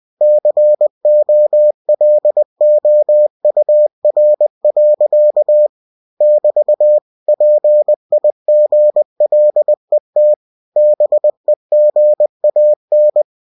Morse code, heard as a keyed tone, about 20 wpm, 600 Hz. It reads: COLOUR. <BT> PIGLET BEGAN